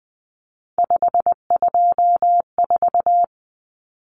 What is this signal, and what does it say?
Morse code, heard as a keyed tone, about 20 wpm, 700 Hz. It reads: 524
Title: Common Cut Numbers - 3-Digit 20wpm